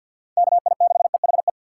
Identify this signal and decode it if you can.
Morse code, heard as a keyed tone, about 46 wpm, 700 Hz. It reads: KI6EHE
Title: USA Call Signs - 6 Characters - 4x Character Spacing - 50wpm